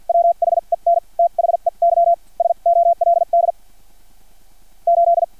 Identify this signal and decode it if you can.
Morse code, heard as a keyed tone, about 50 wpm, spaced as for 41 wpm, 680 Hz. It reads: WHEN THEY SOLD 8
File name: archived / CRSnet